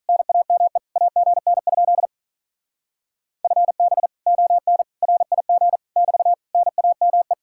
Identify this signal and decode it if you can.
Morse code, heard as a keyed tone, about 45 wpm, 700 Hz. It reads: NAME AGN? FB ON RIG <BT> NAME